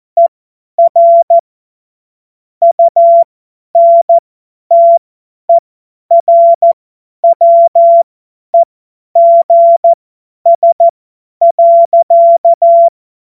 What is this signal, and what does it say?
Morse code, heard as a keyed tone, about 14 wpm, spaced as for 10 wpm, 680 Hz. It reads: ER UNTERWEGS.
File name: Morsekurs / Witze / Witz 14. Februar 2023 50 F70